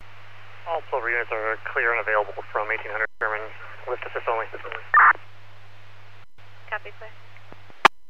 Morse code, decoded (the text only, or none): none